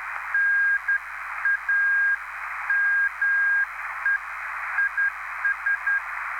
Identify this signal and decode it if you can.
Morse code, heard as a keyed tone, about 10 wpm, 1600 Hz, with noise. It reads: NAME IS
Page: had a nice QSO with SP3JFK on July 1st on 14055 kHz